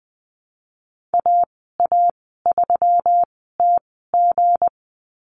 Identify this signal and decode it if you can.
Morse code, heard as a keyed tone, about 20 wpm, spaced as for 15 wpm, 700 Hz. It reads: AA3TG
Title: 186 Lesson 30g - Letter b - Call Signs 20wpm